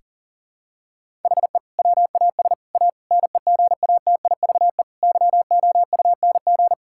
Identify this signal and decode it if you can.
Morse code, heard as a keyed tone, about 40 wpm, 700 Hz. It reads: HE WAS A NEGATIVE YOUNG
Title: Sentences from Top 1000 Words - Review Part 6 - 40wpm